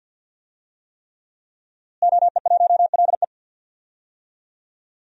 Morse code, heard as a keyed tone, about 46 wpm, 700 Hz. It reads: OE1LE
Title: DXCC-100 Call Signs - 5 Characters - Repeated 3x - 50wpm